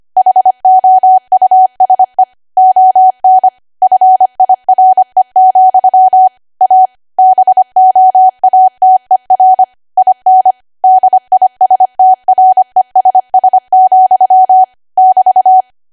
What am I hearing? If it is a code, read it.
Morse code, heard as a keyed tone, about 25 wpm, 750 Hz. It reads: HOUSE ON FIRE, A BOATER IN DISTRESS, <BT>
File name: Archive / 25 WPM / 25WPM